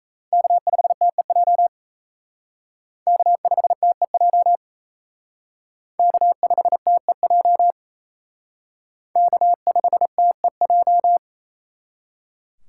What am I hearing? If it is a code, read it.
Morse code, heard as a keyed tone, about 36 wpm, 700 Hz. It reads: K5TEJ K5TEJ K5TEJ K5TEJ